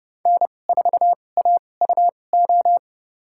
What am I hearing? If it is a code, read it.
Morse code, heard as a keyed tone, about 30 wpm, spaced as for 22 wpm, 700 Hz. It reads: N4AUO
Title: USA Call Signs - 5 Characters - 2x Character Spacing - 30wpm